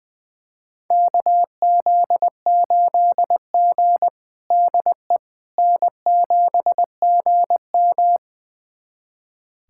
Morse code, heard as a keyed tone, about 20 wpm, 700 Hz. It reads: KZ8G DE N7GM
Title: USA Callsign DE Callsign - 4 Characters - 20wpm